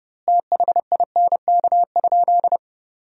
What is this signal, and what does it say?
Morse code, heard as a keyed tone, about 32 wpm, 700 Hz. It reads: THINK?
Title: 2x Word-Spacing - Sentences from Top 200 Words - 30wpm